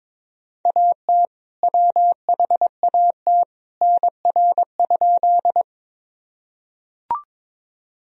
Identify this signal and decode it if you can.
Morse code, heard as a keyed tone, about 22 wpm, 700 Hz. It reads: AT WHAT NR?
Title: Sentences from Top 100 Words and Abbreviations 22wpm